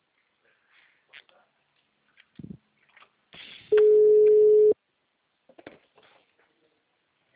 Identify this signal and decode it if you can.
Ringback tone (Europe, 425 Hz), with speech.